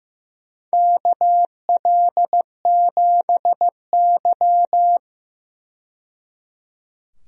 Morse code, heard as a keyed tone, about 15 wpm, 700 Hz. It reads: KL7Y
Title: DXCC-100 Call Signs - 4 Characters - 15wpm